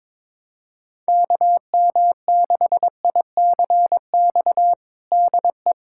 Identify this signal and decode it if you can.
Morse code, heard as a keyed tone, about 22 wpm, 700 Hz. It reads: KM6ICX DE
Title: USA Callsign DE Callsign - 6 Characters - 22wpm